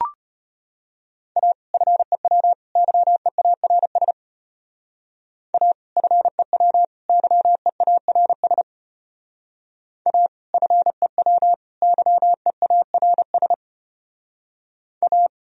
Morse code, heard as a keyed tone, about 31 wpm, 700 Hz. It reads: A FEW YEARS A FEW YEARS A FEW YEARS A
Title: Speed-Racing - Sets of 3 Words 25wpm